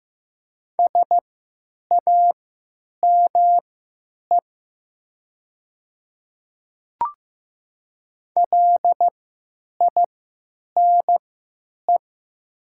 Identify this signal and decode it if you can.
Morse code, heard as a keyed tone, about 15 wpm, spaced as for 9 wpm, 700 Hz. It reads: SAME LINE
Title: 3x Character Spacing - Top 500 Words - 15wpm